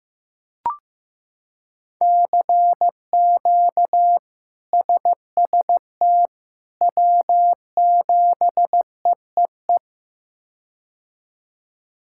Morse code, heard as a keyed tone, about 15 wpm, 700 Hz. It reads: CQ SST W7EEE
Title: K1USN SST QSO Practice - 15wpm